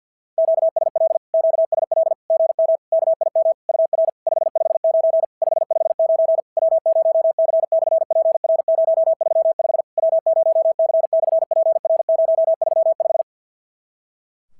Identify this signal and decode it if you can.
Morse code, heard as a keyed tone, about 50 wpm, 650 Hz. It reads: QSL QSL GM KIM UR 559 559 W0C/PR035 W0C/PR035
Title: SOTA QSO Practice 50wpm